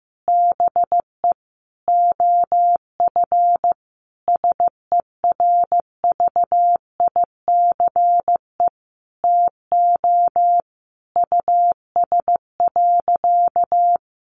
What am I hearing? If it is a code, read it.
Morse code, heard as a keyed tone, about 15 wpm, 700 Hz. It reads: BE OF SERVICE TO US.
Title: Sentences from Top 1200 Words 15wpm